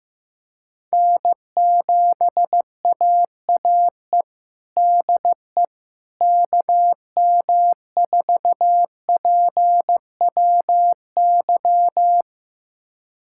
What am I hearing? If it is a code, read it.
Morse code, heard as a keyed tone, about 16 wpm, 700 Hz. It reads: N7AAE DE KM4PWY
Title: USA Callsign DE Callsign 15wpm